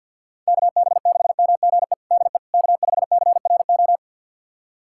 Morse code, heard as a keyed tone, about 50 wpm, 700 Hz. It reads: KB6MGE DE K5CRO
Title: USA Callsign DE Callsign 50wpm